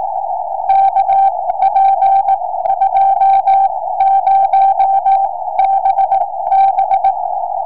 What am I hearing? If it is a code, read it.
Morse code, heard as a keyed tone, about 18 wpm, spaced as for 14 wpm, 780 Hz. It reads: KP20BB
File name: tropo / oh2uhf 70cm